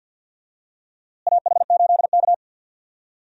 Morse code, heard as a keyed tone, about 51 wpm, 700 Hz. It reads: AH8K
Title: USA Call Signs - 4 Characters - Repeated 3x - 50wpm